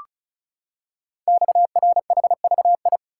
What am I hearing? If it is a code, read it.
Morse code, heard as a keyed tone, about 36 wpm, 690 Hz. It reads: XRHVI
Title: Code Groups - 5 Letters - 35wpm